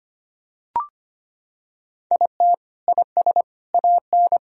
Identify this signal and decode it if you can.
Morse code, heard as a keyed tone, about 25 wpm, 700 Hz. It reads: IT IS AN
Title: Sentences from Top 1000 Words - Review Part 4 - 25wpm